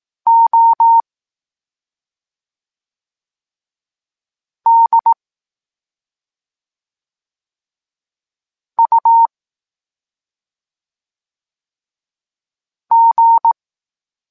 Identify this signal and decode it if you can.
Morse code, heard as a keyed tone, about 18 wpm, 920 Hz. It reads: O D U G
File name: Tekst-23 4